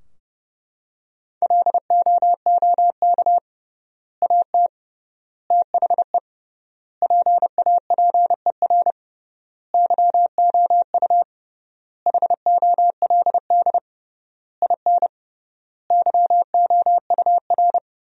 Morse code, heard as a keyed tone, about 32 wpm, 700 Hz. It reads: LOOK AT THE PAPER YOU HOLD IN YOUR
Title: 3x Word-Spacing - Sentences from Top 400 Words - 30wpm